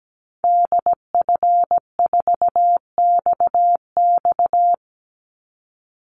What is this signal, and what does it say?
Morse code, heard as a keyed tone, about 17 wpm, 700 Hz. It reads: DF4XX
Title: Call Signs Repeated 3x - 17wpm